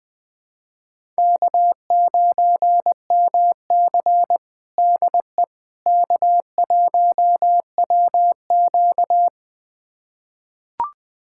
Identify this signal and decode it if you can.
Morse code, heard as a keyed tone, about 20 wpm, 700 Hz. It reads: K9MC DE K1WQ
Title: USA Callsign DE Callsign - 4 Characters - 20wpm